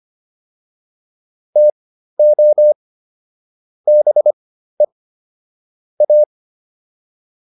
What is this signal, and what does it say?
Morse code, heard as a keyed tone, about 25 wpm, spaced as for 13 wpm, 600 Hz. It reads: TO BE A